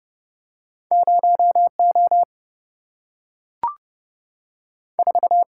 Morse code, heard as a keyed tone, about 31 wpm, 700 Hz. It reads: 0 O 4
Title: Single-Letter Single-Number 30wpm